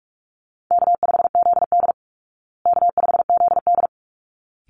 Morse code, heard as a keyed tone, about 44 wpm, 700 Hz. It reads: K5ZD K5ZD
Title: Easy Call Signs Repeated 2x - 45wpm